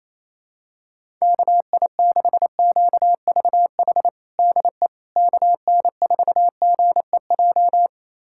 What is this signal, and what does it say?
Morse code, heard as a keyed tone, about 28 wpm, 700 Hz. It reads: KI6QVH DE KN4GEJ
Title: USA Callsign DE Callsign 28wpm